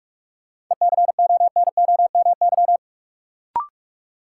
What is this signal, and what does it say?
Morse code, heard as a keyed tone, about 45 wpm, 700 Hz. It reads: ECONOMY
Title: Sentences from Top 1200 Words - New Word Drill Alt - 45wpm